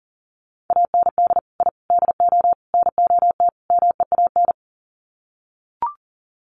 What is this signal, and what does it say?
Morse code, heard as a keyed tone, about 38 wpm, 700 Hz. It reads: AND I DO NOT MEAN
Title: Sets of 5 Words Encore2 40wpm